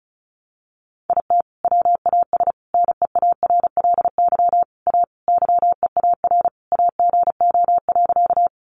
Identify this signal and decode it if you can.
Morse code, heard as a keyed tone, about 34 wpm, 700 Hz. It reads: IT WAS NEARLY A YEAR AGO.
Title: Sentences from Top 1300 Words 35wpm